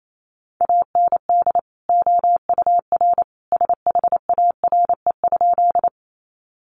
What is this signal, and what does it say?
Morse code, heard as a keyed tone, about 28 wpm, 700 Hz. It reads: AND OUR SHARE?
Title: Sentences from Top 900 Words 28wpm